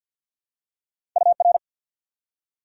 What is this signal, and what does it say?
Morse code, heard as a keyed tone, about 54 wpm, 700 Hz. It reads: UR
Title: Top 90 QSO Elements 50wpm